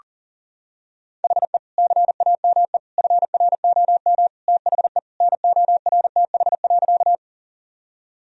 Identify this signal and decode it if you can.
Morse code, heard as a keyed tone, about 40 wpm, 700 Hz. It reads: HE CAME FROM THE NORTH.
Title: Sentences from Top 1000 Words - Review Part 1 - 40wpm